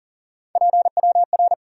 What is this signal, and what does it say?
Morse code, heard as a keyed tone, about 52 wpm, 700 Hz. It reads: PWR